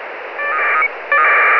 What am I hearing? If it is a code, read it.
Packet radio (AX.25, as in APRS).